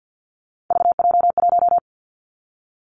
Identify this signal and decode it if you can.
Morse code, heard as a keyed tone, about 50 wpm, 700 Hz. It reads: VJ1
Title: Three Challenging Letter-Number Combinations 50wpm